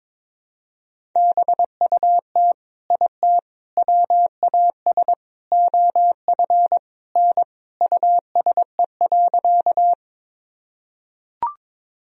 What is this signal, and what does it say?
Morse code, heard as a keyed tone, about 22 wpm, 700 Hz. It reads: BUT IT WAS OF N USE.
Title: Sentences from Top 100 Words and Abbreviations 22wpm